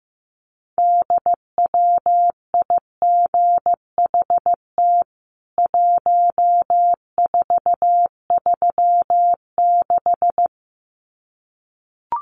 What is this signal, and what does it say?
Morse code, heard as a keyed tone, about 15 wpm, 700 Hz. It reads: DWIGHT 1436
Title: CWT Exchange 15wpm